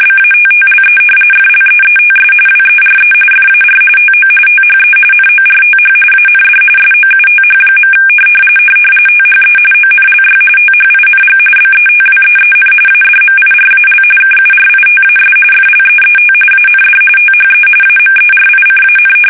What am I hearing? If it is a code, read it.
RTTY, about 45.45 baud. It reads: BATTLE FLEETS WERE FAR TO THE NORTH AND SOUTH. THE ONLY THING BETWEEN ADMIRAL KURITAS FLEET AND THE AMERICAN
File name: fsk850 / battleshipnj